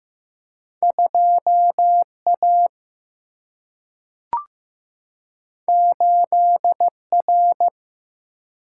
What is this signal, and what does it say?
Morse code, heard as a keyed tone, about 15 wpm, 700 Hz. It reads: 2 A 8 R
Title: Single-Letter Single-Number 15wpm